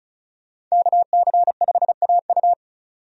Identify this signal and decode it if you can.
Morse code, heard as a keyed tone, about 36 wpm, 700 Hz. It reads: KC5AU
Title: USA Call Signs - 5 Characters - 35wpm